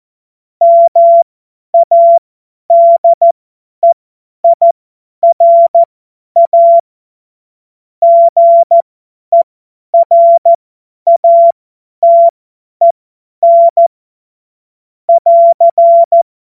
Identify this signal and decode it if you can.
Morse code, heard as a keyed tone, about 14 wpm, spaced as for 10 wpm, 680 Hz. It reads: MADEIRA GERATEN <AR>
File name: Morsekurs / Witze / Witz 23. August 2022 50 F70